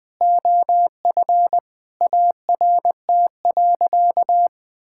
Morse code, heard as a keyed tone, about 21 wpm, 700 Hz. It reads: OF ART.
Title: Sentences with 4-Word Phrases - 20wpm